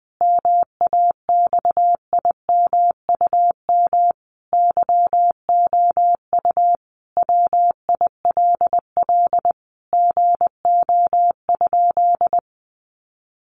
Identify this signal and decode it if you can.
Morse code, heard as a keyed tone, about 19 wpm, 700 Hz. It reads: MAXIMUM YOU WILL GO?
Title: Sentences from Top 1300 Words 20wpm